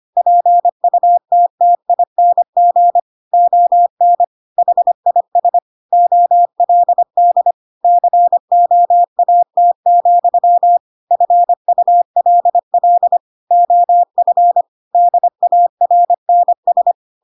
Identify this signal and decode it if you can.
Morse code, heard as a keyed tone, about 25 wpm, 700 Hz. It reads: PUTTING ON HIS OLD COAT, FULL OF DARNS